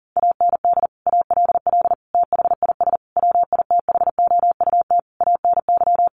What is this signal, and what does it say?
Morse code, heard as a keyed tone, about 40 wpm, 700 Hz. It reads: AND ALL THIS WITHOUT ANY
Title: Sentences from Top 1200 Words 40wpm